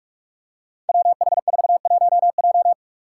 Morse code, heard as a keyed tone, about 42 wpm, 700 Hz. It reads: WH41J
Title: Five Challenging Letter-Number Combinations 45wpm